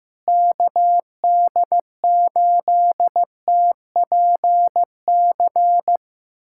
Morse code, heard as a keyed tone, about 16 wpm, 700 Hz. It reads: KD8TPC
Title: USA Callsign DE Callsign - 6 Characters - 15wpm